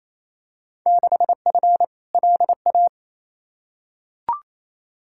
Morse code, heard as a keyed tone, about 28 wpm, 700 Hz. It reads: 6F LA
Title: Field Day Exchanges Encore - 28wpm